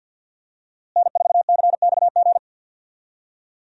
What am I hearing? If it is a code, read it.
Morse code, heard as a keyed tone, about 52 wpm, 700 Hz. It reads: N4CXG
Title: USA Call Signs - 5 Characters - Repeated 3x - 50wpm